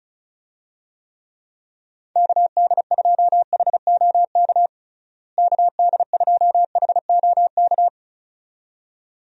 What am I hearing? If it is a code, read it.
Morse code, heard as a keyed tone, about 34 wpm, 700 Hz. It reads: KD2HOK KD2HOK
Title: USA Call Signs - 6 Characters - Repeated 2x - 35wpm